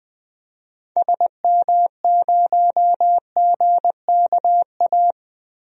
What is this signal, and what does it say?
Morse code, heard as a keyed tone, about 20 wpm, 700 Hz. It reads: SM0GKA
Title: DXCC-100 Call Signs - 6 Characters - Repeated 3x - 20wpm